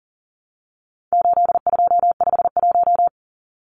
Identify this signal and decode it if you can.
Morse code, heard as a keyed tone, about 40 wpm, 700 Hz. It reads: 8251
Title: Numbers - 4-Digit 40wpm